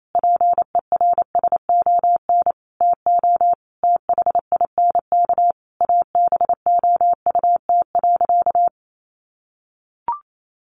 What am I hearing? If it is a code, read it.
Morse code, heard as a keyed tone, about 28 wpm, 700 Hz. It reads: PERSON TO THINK ABOUT.
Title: Sentences from Top 1100 Words 28wpm